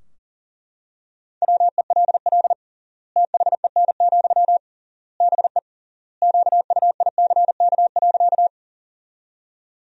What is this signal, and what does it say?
Morse code, heard as a keyed tone, about 40 wpm, 700 Hz. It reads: WELL THEN, BE QUICK.
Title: 3x Word-Spacing - Sentences from Top 500 Words - 40wpm